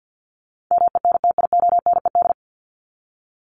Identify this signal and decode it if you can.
Morse code, heard as a keyed tone, about 49 wpm, 700 Hz. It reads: MENTIONED
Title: ED Suffix - Top 100 Words with Base Words 50wpm